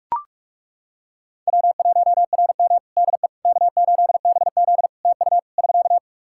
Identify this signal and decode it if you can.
Morse code, heard as a keyed tone, about 41 wpm, 700 Hz. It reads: W1RM DE K8BZ TU <SK>
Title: Practice QSO - Exchange 4 - Ending - 45wpm